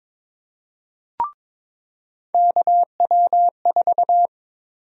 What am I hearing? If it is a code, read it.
Morse code, heard as a keyed tone, about 22 wpm, 700 Hz. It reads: KW4
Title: USA Call Sign Prefixes - 22wpm